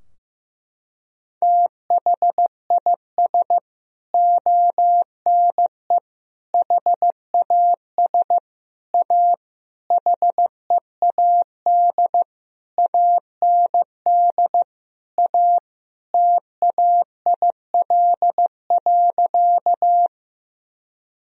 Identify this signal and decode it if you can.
Morse code, heard as a keyed tone, about 16 wpm, 700 Hz. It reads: THIS ONE HAS A HEAD AND A TAIL.
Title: Sentences from Top 1000 Words - Review Part 3 - 15wpm